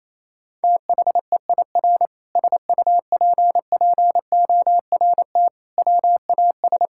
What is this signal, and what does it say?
Morse code, heard as a keyed tone, about 28 wpm, 700 Hz. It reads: THEIR SUPPORT WAS
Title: Sentences from Top 1000 Words - Review Part 6 - 28wpm